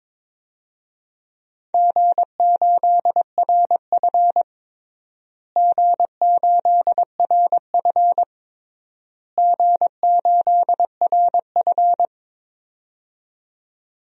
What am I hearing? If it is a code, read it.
Morse code, heard as a keyed tone, about 22 wpm, 700 Hz. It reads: G8RF G8RF G8RF
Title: DXCC-100 Call Signs - 4 Characters - Repeated 3x - 22wpm